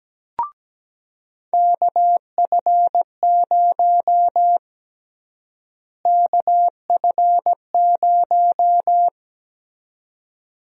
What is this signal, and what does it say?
Morse code, heard as a keyed tone, about 17 wpm, 700 Hz. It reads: KF0 KF0
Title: USA Call Sign Prefixes - Repeated 2x - 17wpm